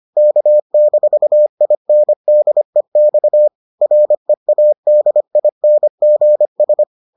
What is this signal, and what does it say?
Morse code, heard as a keyed tone, about 25 wpm, 600 Hz. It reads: K-INDEX READINGS